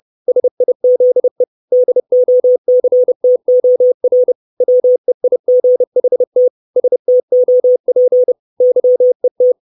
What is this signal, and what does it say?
Morse code, heard as a keyed tone, about 30 wpm, 500 Hz. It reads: SIZE DOCTOR WEIGHT STOP YET
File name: 632 Words 30WPM